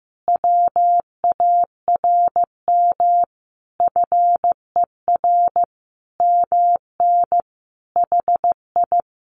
Morse code, heard as a keyed tone, about 15 wpm, 700 Hz. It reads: WARM FER MN HI
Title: Common QSO Phrases Encore 15wpm